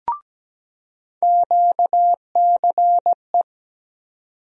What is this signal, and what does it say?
Morse code, heard as a keyed tone, about 17 wpm, 700 Hz. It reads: QCE
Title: USA Call Sign Postfixes - 3 Letters 17wpm